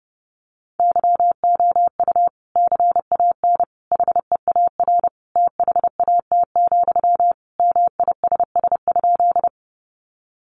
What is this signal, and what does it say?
Morse code, heard as a keyed tone, about 30 wpm, 700 Hz. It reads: YOU CAN HEAR THAT, MISS?
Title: Sentences from Top 500 Words 30wpm